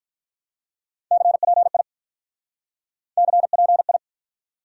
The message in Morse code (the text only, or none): CPI CPI